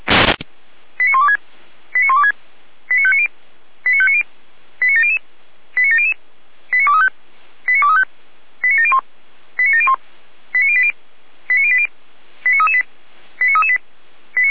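Five-tone selective call (ZVEI), heard as quick runs of five tones, tones of about 80 ms. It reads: 89126 89126 89590 89590 89700 89700 89235 89235 89701 89701 89080 89080 89338 89338
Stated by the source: ZVEI five-tone selective call received with an SDR.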